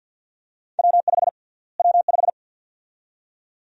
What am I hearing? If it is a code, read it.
Morse code, heard as a keyed tone, about 52 wpm, 700 Hz. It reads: W5 W5
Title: USA Call Sign Prefixes - Repeated 2x - 50wpm